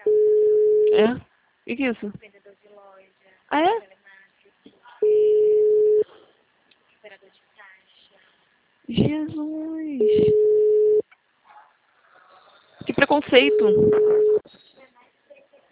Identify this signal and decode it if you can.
Ringback tone (Europe, 425 Hz), with speech.
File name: out- -2015- - - .11983